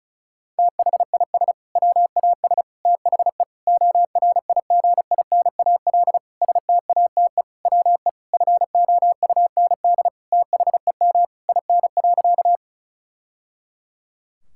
Morse code, heard as a keyed tone, about 36 wpm, 700 Hz. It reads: THIS WAS THE ORIGINAL STATE WE FOUND THEM IN.
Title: Sentences from Top 1000 Words - Review Part 6 - 35wpm